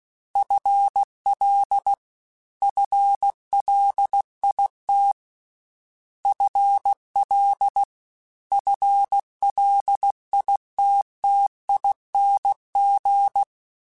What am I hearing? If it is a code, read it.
Morse code, heard as a keyed tone, about 16 wpm, 800 Hz. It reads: FL FLIT FL FLITTING
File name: FL / FL mp3s / FL Prefixes 1x1 @15